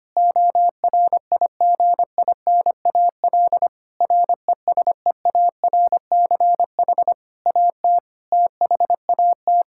Morse code, heard as a keyed tone, about 25 wpm, 700 Hz. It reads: ORIGINAL RESEARCH AT THAT
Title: Sentences from Top 1000 Words - Review Part 6 - 25wpm